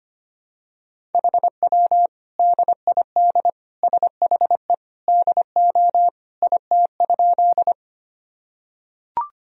Morse code, heard as a keyed tone, about 25 wpm, 700 Hz. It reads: HW DID SHE DO IT?
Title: Sentences from Top 100 Words and Abbreviations 25wpm